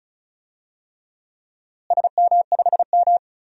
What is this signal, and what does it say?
Morse code, heard as a keyed tone, about 36 wpm, 700 Hz. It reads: SM5M
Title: DXCC-100 Call Signs - 4 Characters - Repeated 3x - 35wpm